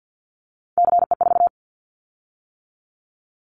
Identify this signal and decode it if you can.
Morse code, heard as a keyed tone, about 50 wpm, 700 Hz. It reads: CE4
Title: Callsign Prefixes for Top 100 DXCC - 50wpm